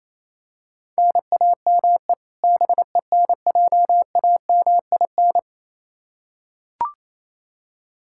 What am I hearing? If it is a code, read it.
Morse code, heard as a keyed tone, about 27 wpm, 700 Hz. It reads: NAME BENJAMIN
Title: K1USN SST QSO Practice 28wpm